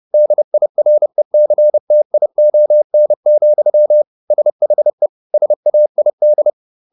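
Morse code, heard as a keyed tone, about 30 wpm, 600 Hz. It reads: DIRECTION, SHE SAID
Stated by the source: Morse code at 30 wpm, the novel Never by Ken Follett in English, from CW med Gnister.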